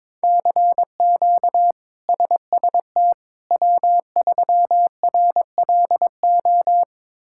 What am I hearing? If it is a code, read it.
Morse code, heard as a keyed tone, about 22 wpm, 700 Hz. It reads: CQ SST W3RLO
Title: K1USN SST QSO Practice - 22wpm